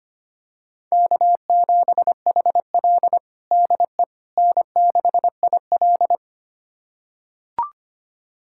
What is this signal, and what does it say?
Morse code, heard as a keyed tone, about 25 wpm, 700 Hz. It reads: K7HL DE N6IL
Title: USA Callsign DE Callsign - 4 Characters - 25wpm